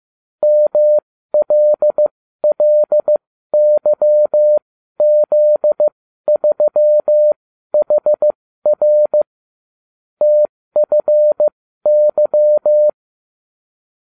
Morse code, heard as a keyed tone, about 15 wpm, spaced as for 12 wpm, 600 Hz. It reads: MLLYZ3HR TFY